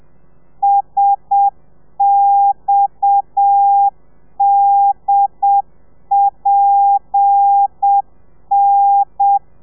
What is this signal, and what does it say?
Morse code, heard as a keyed tone, about 7 wpm, 800 Hz. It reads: SXDPN
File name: telegrafia / i2tfj / Lez7 30